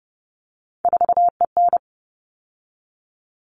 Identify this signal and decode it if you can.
Morse code, heard as a keyed tone, about 30 wpm, 700 Hz. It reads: 4EN
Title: All Cut Numbers - 3-Digit 30wpm